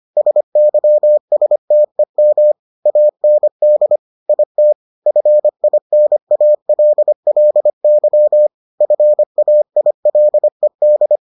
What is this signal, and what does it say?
Morse code, heard as a keyed tone, about 25 wpm, 600 Hz. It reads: SYSTEM AND IT FINALLY FAILED